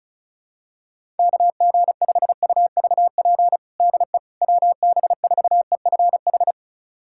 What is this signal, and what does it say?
Morse code, heard as a keyed tone, about 34 wpm, 700 Hz. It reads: KG5UVP DE WB4EFH
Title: USA Callsign DE Callsign 35wpm